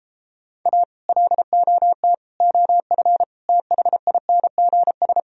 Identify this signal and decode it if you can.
Morse code, heard as a keyed tone, about 33 wpm, 700 Hz. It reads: A LOT OF THINGS